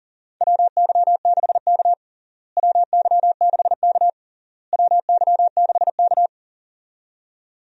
Morse code, heard as a keyed tone, about 40 wpm, 700 Hz. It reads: WY6K WY6K WY6K
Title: USA Call Signs - 4 Characters - Repeated 3x - 40wpm